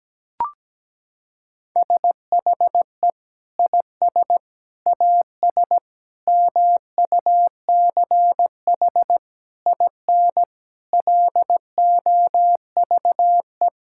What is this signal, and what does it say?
Morse code, heard as a keyed tone, about 17 wpm, 700 Hz. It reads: SHE IS AS MUCH IN LOVE
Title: Sentences from Top 1000 Words - Review Part 5 - 17wpm